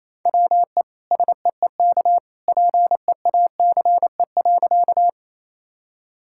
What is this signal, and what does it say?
Morse code, heard as a keyed tone, about 28 wpm, 700 Hz. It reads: WE SEEK PEACE.
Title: Sentences from Top 1600 Words 28wpm